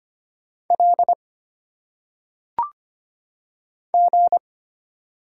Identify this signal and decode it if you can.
Morse code, heard as a keyed tone, about 25 wpm, 700 Hz. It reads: L G
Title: Single Letters - Phonetic Pronunciation 25wpm